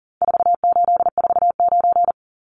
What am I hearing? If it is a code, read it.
Morse code, heard as a keyed tone, about 40 wpm, 700 Hz. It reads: 4849
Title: Numbers - 4-Digit 40wpm